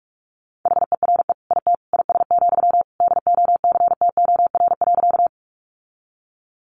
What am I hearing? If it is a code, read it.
Morse code, heard as a keyed tone, about 44 wpm, 700 Hz. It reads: HERE IT IS, DOCTOR.
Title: Sentences from Top 1500 Words 45wpm